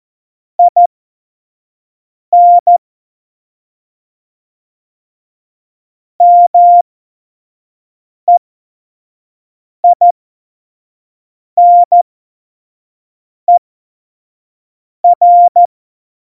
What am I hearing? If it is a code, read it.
Morse code, heard as a keyed tone, about 14 wpm, spaced as for 5 wpm, 700 Hz. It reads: IN MEINER
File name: Morsekurs / Witze / Witz 10. September 2019, 25 F70